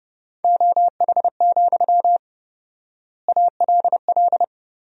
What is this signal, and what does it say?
Morse code, heard as a keyed tone, about 30 wpm, 700 Hz. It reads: OH, ALL
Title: 4x Word-Spacing - Sentences from Top 500 Words - 30wpm